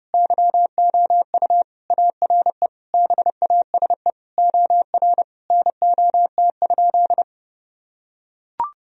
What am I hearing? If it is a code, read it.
Morse code, heard as a keyed tone, about 30 wpm, 700 Hz. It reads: YOU ARE BASE OR NOT?
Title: Sentences from Top 1000 Words - Review Part 1 - 30wpm